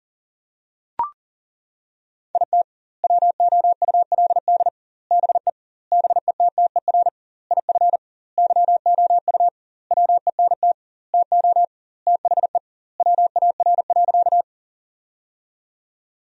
Morse code, heard as a keyed tone, about 37 wpm, 700 Hz. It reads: IT WOULD BE BETTER IF YOU WENT TO THE WAR.
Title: 2x Word-Spacing - Sentences from Top 400 Words - 40wpm